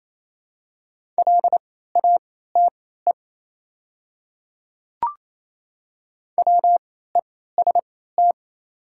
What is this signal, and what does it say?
Morse code, heard as a keyed tone, about 28 wpm, spaced as for 16 wpm, 690 Hz. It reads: LATE WEST
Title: 3x Character Spacing - Top 500 Words - 28wpm